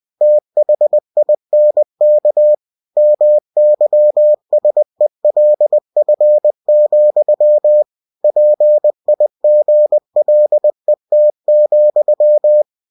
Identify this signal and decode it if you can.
Morse code, heard as a keyed tone, about 20 wpm, 600 Hz. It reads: THINK MYSELF, PIGLET,
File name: cw-books / winnie-the-pooh / winnie-20wpm-page-0034